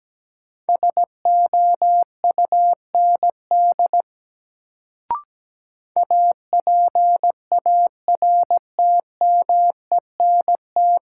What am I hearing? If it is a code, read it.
Morse code, heard as a keyed tone, about 17 wpm, 700 Hz. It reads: SOUND APARTMENT
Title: Sentences from Top 900 Words - New Word Drill - 17wpm